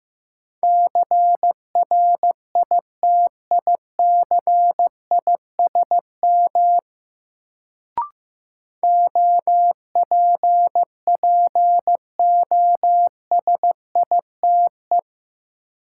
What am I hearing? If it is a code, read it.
Morse code, heard as a keyed tone, about 15 wpm, 700 Hz. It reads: CRITICISM OPPOSITE
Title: Sentences from Top 1300 Words - New Word Drill - 15wpm